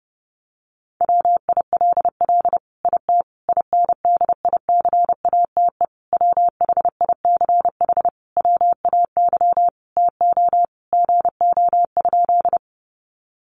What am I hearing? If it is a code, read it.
Morse code, heard as a keyed tone, about 30 wpm, 700 Hz. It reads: WILL IT INDICATE WHICH WAY TO GO?
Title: Sentences from Top 1300 Words 30wpm